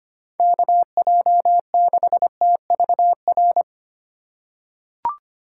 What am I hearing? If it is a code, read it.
Morse code, heard as a keyed tone, about 25 wpm, 700 Hz. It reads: KJ6TVR
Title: USA Call Signs - 6 Characters - 6x Character Spacing - 25wpm